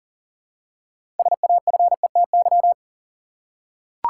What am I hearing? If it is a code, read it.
Morse code, heard as a keyed tone, about 40 wpm, 700 Hz. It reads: SAFETY
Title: Six Letter Words - 40wpm